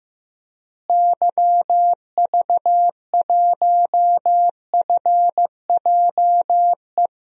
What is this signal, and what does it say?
Morse code, heard as a keyed tone, about 15 wpm, 700 Hz. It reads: YV1FJE
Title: DXCC-100 Call Signs - 6 Characters - Repeated 3x - 15wpm